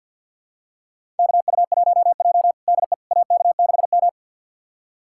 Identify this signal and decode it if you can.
Morse code, heard as a keyed tone, about 51 wpm, 690 Hz. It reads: KU1J DE AK6M
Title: USA Callsign DE Callsign - 4 Characters - 50wpm